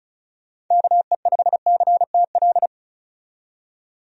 Morse code, heard as a keyed tone, about 34 wpm, 700 Hz. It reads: KE5CTL
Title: USA Call Signs - 6 Characters - 35wpm